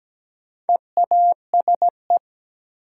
Morse code, heard as a keyed tone, about 17 wpm, 700 Hz. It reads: EASE
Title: Sentences from Top 1000 Words - Review Part 1 - New Word Drill - 17wpm